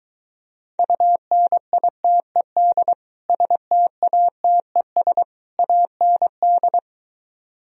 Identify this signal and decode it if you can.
Morse code, heard as a keyed tone, about 23 wpm, 700 Hz. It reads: UNITED STATES AND